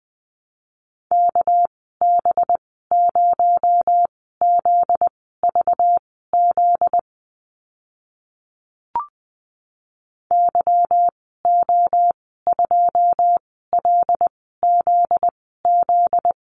Morse code, heard as a keyed tone, about 20 wpm, spaced as for 14 wpm, 700 Hz. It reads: KB0ZVZ YO2LZZ
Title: 245 Lesson 39g - Letter z - Call Signs 20wpm